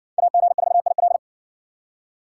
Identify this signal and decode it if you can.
Morse code, heard as a keyed tone, about 53 wpm, 700 Hz. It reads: AG4IL